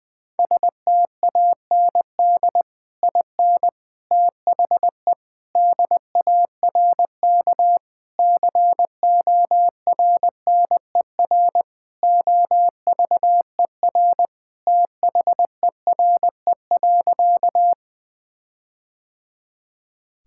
Morse code, heard as a keyed tone, about 20 wpm, 700 Hz. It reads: STAND IN THE DARK CORNER OVER THERE.
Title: Sentences from Top 1600 Words 20wpm